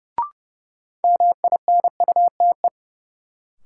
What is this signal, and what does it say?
Morse code, heard as a keyed tone, about 29 wpm, 700 Hz. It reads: MINUTE
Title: Sentences from Top 500 Words - New Word Drill - 30wpm